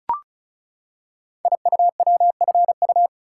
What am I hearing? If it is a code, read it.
Morse code, heard as a keyed tone, about 36 wpm, 700 Hz. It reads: IUWFU
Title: Code Groups - 5 Letters - 35wpm